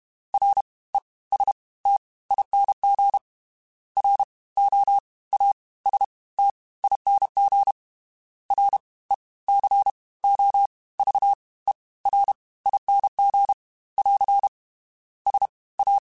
Morse code, heard as a keyed tone, about 32 wpm, spaced as for 20 wpm, 800 Hz. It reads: RESTING ROASTING RECOVERING <AR> SA
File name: Special INGs / Special INGs-mp3s / Fast ing @30